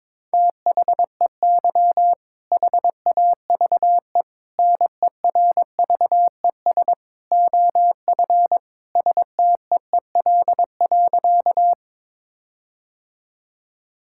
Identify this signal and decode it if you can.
Morse code, heard as a keyed tone, about 22 wpm, 690 Hz. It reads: THEY HAVE NERVES OF STEEL.
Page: morse-code-ninja-practice-22wpm